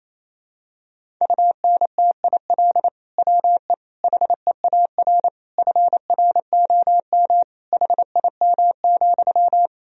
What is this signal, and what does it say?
Morse code, heard as a keyed tone, about 28 wpm, 700 Hz. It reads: UNTIL WE HEAR FROM HIM,
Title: Sentences from Top 1000 Words - Review Part 1 - 28wpm